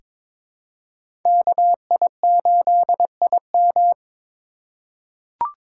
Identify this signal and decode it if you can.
Morse code, heard as a keyed tone, about 22 wpm, 700 Hz. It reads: KI8IM
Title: USA Call Signs - 5 Characters - 3x Character Spacing - 22wpm